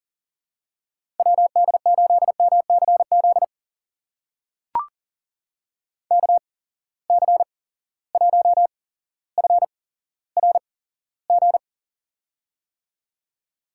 Morse code, heard as a keyed tone, about 40 wpm, spaced as for 17 wpm, 700 Hz. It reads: WD8MCZ KC1FRG
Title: USA Call Signs - 6 Characters - 8x Character Spacing - 40wpm